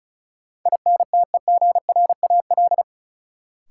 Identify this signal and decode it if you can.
Morse code, heard as a keyed tone, about 36 wpm, 700 Hz. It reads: INTEGRAL
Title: Sets of 5 Words - New Word Drill - 35wpm